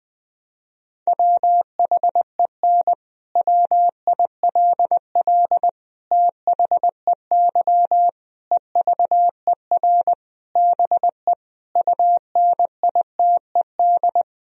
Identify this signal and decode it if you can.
Morse code, heard as a keyed tone, about 21 wpm, 700 Hz. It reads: WHEN WILL THEY EVER BE UNITED
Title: Sentences from Top 1000 Words - Review Part 4 - 20wpm